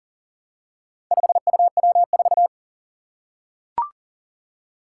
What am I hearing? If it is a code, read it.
Morse code, heard as a keyed tone, about 37 wpm, 700 Hz. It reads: 5UW4
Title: Four Challenging Letter-Number Combinations 40wpm